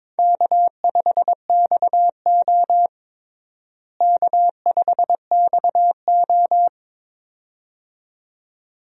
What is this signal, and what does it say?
Morse code, heard as a keyed tone, about 22 wpm, 700 Hz. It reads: K5XO K5XO
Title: USA Call Signs - 4 Characters - Repeated 2x - 22wpm